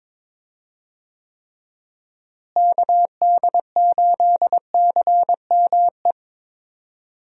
Morse code, heard as a keyed tone, about 22 wpm, 700 Hz. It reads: KD8CME
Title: USA Call Signs - 6 Characters - Repeated 3x - 22wpm